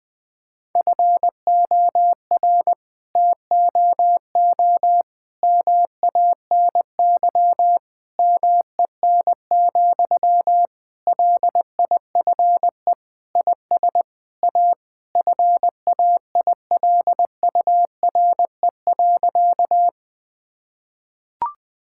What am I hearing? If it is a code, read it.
Morse code, heard as a keyed tone, about 20 wpm, 700 Hz. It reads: FOR TOO MANY MEN, LIFE IS A FAILURE.
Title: Sentences from Top 1000 Words - Review Part 6 - 20wpm